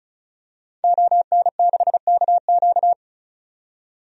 Morse code, heard as a keyed tone, about 36 wpm, 700 Hz. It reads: ON6KQ
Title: DXCC-100 Call Signs - 5 Characters - 35wpm